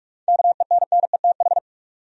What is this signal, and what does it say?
Morse code, heard as a keyed tone, about 46 wpm, 700 Hz. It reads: KENNETH
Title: USA Common Names - 45wpm